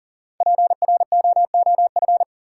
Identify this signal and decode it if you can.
Morse code, heard as a keyed tone, about 40 wpm, 700 Hz. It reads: PROOF
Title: Five Letter Words - 40wpm